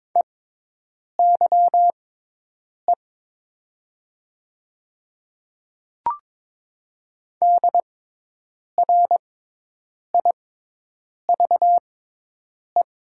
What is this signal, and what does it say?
Morse code, heard as a keyed tone, about 22 wpm, spaced as for 8 wpm, 700 Hz. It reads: EYE DRIVE
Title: 6x Character Spacing - Top 500 Words - 22wpm